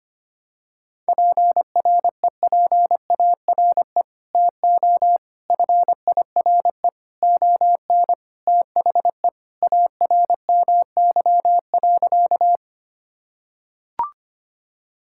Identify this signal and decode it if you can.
Morse code, heard as a keyed tone, about 25 wpm, 700 Hz. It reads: PREPARE TO FIRE ON THE ARMY.
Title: Sentences from Top 1000 Words - Review Part 6 - 25wpm